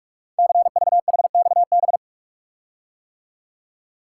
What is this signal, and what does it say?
Morse code, heard as a keyed tone, about 44 wpm, 700 Hz. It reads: CVHXB
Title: Code Groups - 5 Letters - 45wpm